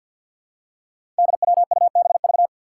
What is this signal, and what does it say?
Morse code, heard as a keyed tone, about 52 wpm, 700 Hz. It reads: DWUBV
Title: Five Challenging Letter-Number Combinations 50wpm